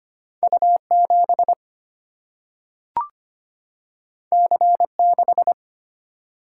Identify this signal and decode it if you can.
Morse code, heard as a keyed tone, about 25 wpm, 700 Hz. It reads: U 7 C 6
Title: Single-Letter Single-Number 25wpm